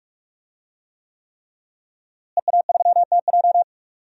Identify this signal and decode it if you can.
Morse code, heard as a keyed tone, about 48 wpm, 700 Hz. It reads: EA3TJ
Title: DXCC-100 Call Signs - 5 Characters - Repeated 3x - 45wpm